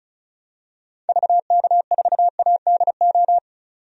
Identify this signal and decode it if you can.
Morse code, heard as a keyed tone, about 36 wpm, 700 Hz. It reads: VK4ADO
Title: DXCC-100 Call Signs - 6 Characters - 35wpm